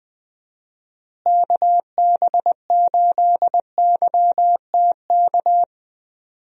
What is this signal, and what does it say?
Morse code, heard as a keyed tone, about 20 wpm, 700 Hz. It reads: KB8YTK
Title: USA Call Signs - 6 Characters - Repeated 2x - 20wpm